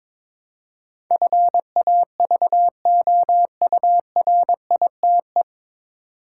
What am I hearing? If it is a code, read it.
Morse code, heard as a keyed tone, about 22 wpm, 700 Hz. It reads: FAVOURITE
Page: Morse Code Ninja practice 22wpm